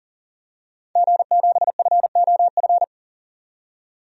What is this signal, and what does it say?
Morse code, heard as a keyed tone, about 40 wpm, 700 Hz. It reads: G7FOF
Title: DXCC-100 Call Signs - 5 Characters - 40wpm